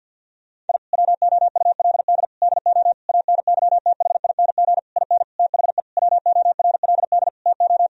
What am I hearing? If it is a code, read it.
Morse code, heard as a keyed tone, about 50 wpm, 700 Hz. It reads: I WOULD DO ANYTHING IN THE WORLD TO
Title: Sentences from Top 1000 Words - Review Part 2 - 50wpm